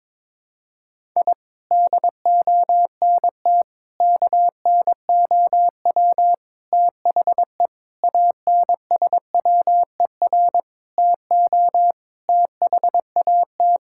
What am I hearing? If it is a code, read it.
Morse code, heard as a keyed tone, about 22 wpm, 700 Hz. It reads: I DONT KNOW THE ANSWER TO THAT
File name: practice / 22wpm / Sentences with 7-Word Phrases - 22wpm